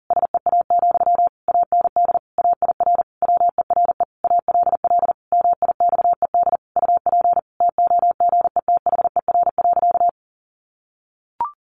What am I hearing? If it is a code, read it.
Morse code, heard as a keyed tone, about 40 wpm, 700 Hz. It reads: SEA, AND AIR WERE ALL MIXED UP TOGETHER.
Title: Sentences from Top 1500 Words 40wpm